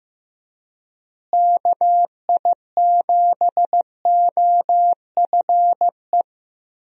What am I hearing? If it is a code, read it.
Morse code, heard as a keyed tone, about 16 wpm, 700 Hz. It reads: KI7OFE
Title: USA Call Signs - 6 Characters - Repeated 3x - 15wpm